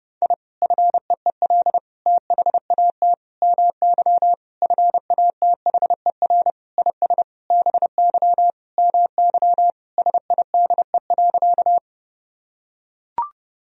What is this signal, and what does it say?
Morse code, heard as a keyed tone, about 32 wpm, 700 Hz. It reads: I FEEL THAT MY FATHER IS BY MY SIDE.
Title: Sentences from Top 1000 Words - Review Part 1 - 30wpm